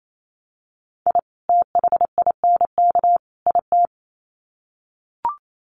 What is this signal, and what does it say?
Morse code, heard as a keyed tone, about 28 wpm, 700 Hz. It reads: I THINK IT
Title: Sets of 3 Words 28wpm